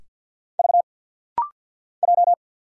Morse code, heard as a keyed tone, about 50 wpm, 700 Hz. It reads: V J
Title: Single Letters - Rapid-Fire 50wpm